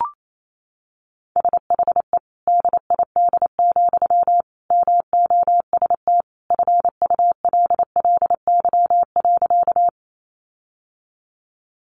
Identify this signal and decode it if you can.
Morse code, heard as a keyed tone, about 28 wpm, 700 Hz. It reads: SHE DID, MOST FULLY.
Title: Sentences from Top 1300 Words 28wpm